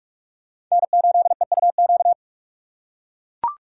Morse code, heard as a keyed tone, about 43 wpm, 700 Hz. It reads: N8EUQ
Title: USA Call Signs - 5 Characters - 2x Character Spacing - 45wpm